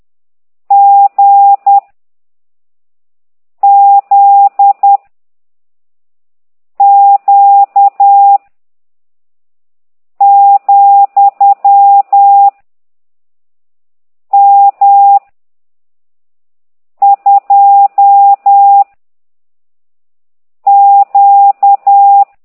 Morse code, heard as a keyed tone, about 10 wpm, 800 Hz. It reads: G Z Q , M 2 Q
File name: ct2idw / exerc13